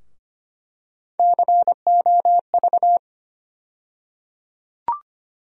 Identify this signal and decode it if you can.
Morse code, heard as a keyed tone, about 25 wpm, 700 Hz. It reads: COV
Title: USA Call Sign Postfixes - 3 Letters 25wpm